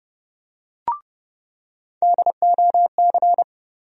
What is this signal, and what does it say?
Morse code, heard as a keyed tone, about 30 wpm, 700 Hz. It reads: DOC
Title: USA Call Sign Postfixes - 3 Letters 30wpm